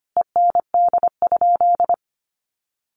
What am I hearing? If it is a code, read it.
Morse code, heard as a keyed tone, about 25 wpm, 700 Hz. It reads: END?
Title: Sentences from Top 1200 Words 25wpm